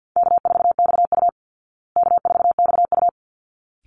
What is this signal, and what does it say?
Morse code, heard as a keyed tone, about 48 wpm, 700 Hz. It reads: K4XU K4XU
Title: Easy Call Signs Repeated 2x - 50wpm